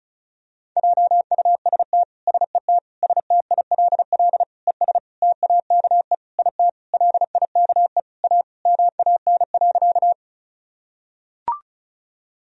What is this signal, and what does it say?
Morse code, heard as a keyed tone, about 36 wpm, 700 Hz. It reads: JUST SET STILL ES TAKE IT LIKE A MAN.
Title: Sentences from Top 300 Words and Abbreviations 35wpm